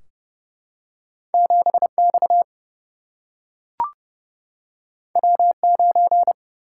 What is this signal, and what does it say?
Morse code, heard as a keyed tone, about 30 wpm, 700 Hz. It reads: 7X W9
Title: Single-Letter Single-Number 30wpm